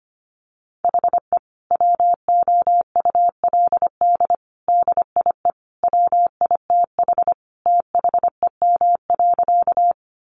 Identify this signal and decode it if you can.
Morse code, heard as a keyed tone, about 25 wpm, 700 Hz. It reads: HE WOULD DIE WITH THEM.
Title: Sentences from Top 1300 Words 25wpm